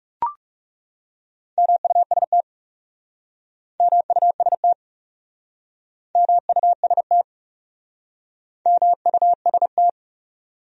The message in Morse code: MUST MUST MUST MUST

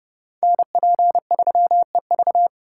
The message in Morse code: NP3EV

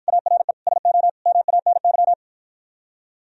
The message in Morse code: ARE SO MANY

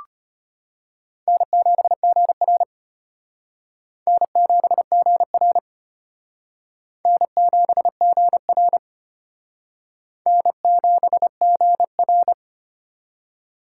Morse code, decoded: N7GR N7GR N7GR N7GR

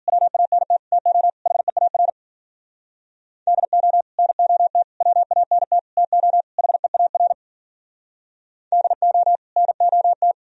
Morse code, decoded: WANT TO HEAR DO NOT WANT TO HEAR DO NOT

